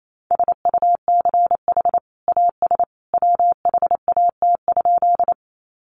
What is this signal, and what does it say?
Morse code, heard as a keyed tone, about 28 wpm, 700 Hz. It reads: SUCH AS WHAT?
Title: Sentences from Top 200 Words 28wpm